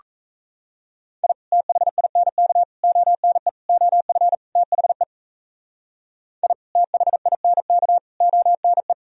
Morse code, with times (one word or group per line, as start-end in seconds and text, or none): 1.2-1.3 I
1.5-2.6 THINK
2.8-3.5 ONE
3.7-4.3 OF
4.6-5.0 THE
6.4-6.5 I
6.8-8.0 THINK
8.2-8.9 ONE